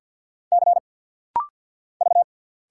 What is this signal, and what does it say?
Morse code, heard as a keyed tone, about 50 wpm, 700 Hz. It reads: C V
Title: Single Letter-Number - Rapid-Fire 50wpm